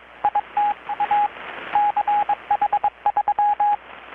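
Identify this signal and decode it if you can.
Morse code, heard as a keyed tone, about 21 wpm, 830 Hz, with radio noise. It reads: ITA CH3
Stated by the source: Morse from KPH's Night of Nights XX maritime radiotelegraphy event, received in Chiapas, Mexico about 3,800 km away on a Kenwood R-600 with a grounded 10 m long-wire antenna.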